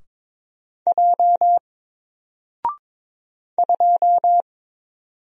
Morse code, heard as a keyed tone, about 22 wpm, 700 Hz. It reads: J 2
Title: Single Character (Letter:Number:Punctuation:Prosign) 22wpm